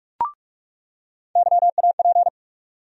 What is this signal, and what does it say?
Morse code, heard as a keyed tone, about 43 wpm, 700 Hz. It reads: YAP
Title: USA Call Sign Postfixes - 3 Letters 45wpm